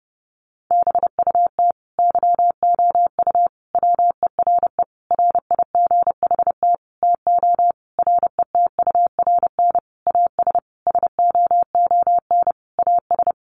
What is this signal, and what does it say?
Morse code, heard as a keyed tone, about 28 wpm, 700 Hz. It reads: BUT YOU WERE RIGHT TO RETURN AS SOON AS
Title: Sentences from Top 800 Words 30wpm